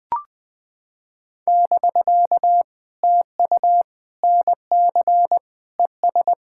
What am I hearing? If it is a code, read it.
Morse code, heard as a keyed tone, about 20 wpm, 700 Hz. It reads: <BK> TU NC ES
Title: POTA QSO Practice - 20wpm